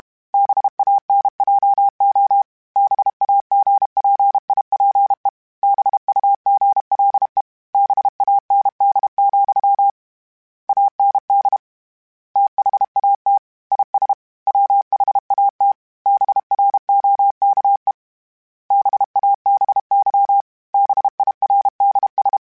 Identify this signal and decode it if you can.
Morse code, heard as a keyed tone, about 32 wpm, 800 Hz. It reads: BANJO BAGPIPE BUGLE BAND, AND THAT IS WHAT BROKE BABY BIRDS